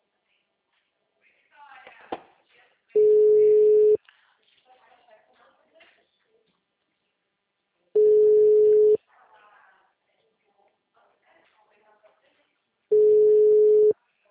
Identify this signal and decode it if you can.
Ringback tone (Europe, 425 Hz), with speech.